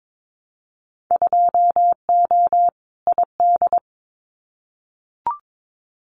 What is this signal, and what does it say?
Morse code, heard as a keyed tone, about 22 wpm, 700 Hz. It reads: 2O ID
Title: Winter Field Day Exchanges 22wpm